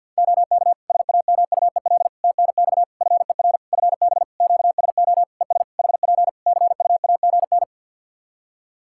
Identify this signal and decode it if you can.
Morse code, heard as a keyed tone, about 51 wpm, 700 Hz. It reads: OK SAMUEL TNX FER FB QSO ES HP CUAGN
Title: K1USN SST QSO Practice 50wpm